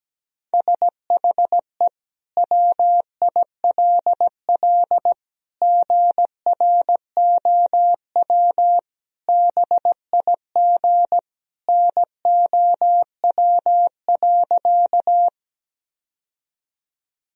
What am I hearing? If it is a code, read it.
Morse code, heard as a keyed tone, about 17 wpm, 700 Hz. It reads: SHE WILL GROW BIG NOW.
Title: Sentences from Top 1000 Words - Review Part 1 - 17wpm